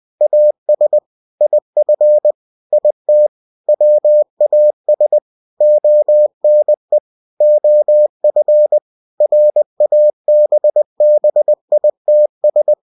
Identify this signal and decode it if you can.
Morse code, heard as a keyed tone, about 20 wpm, 600 Hz. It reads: AS IF IT WAS ONE OF RABBITS
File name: cw-books / winnie-the-pooh / winnie-20wpm-page-0107